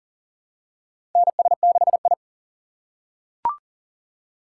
Morse code, heard as a keyed tone, about 40 wpm, 700 Hz. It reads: NS6I
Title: USA Call Signs - 4 Characters - 6x Character Spacing - 40wpm